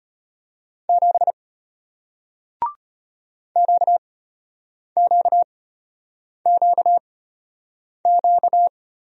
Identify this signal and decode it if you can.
Morse code, heard as a keyed tone, about 33 wpm, spaced as for 8 wpm, 700 Hz. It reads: 7 QQQQ